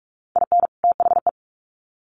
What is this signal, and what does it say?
Morse code, heard as a keyed tone, about 44 wpm, 700 Hz. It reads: IN THE
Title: Sets of 2 Words 45wpm